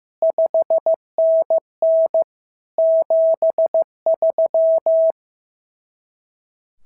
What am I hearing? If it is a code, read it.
Morse code, heard as a keyed tone, about 15 wpm, 650 Hz. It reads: 5NN 73
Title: SOTA QSO Practice 15wpm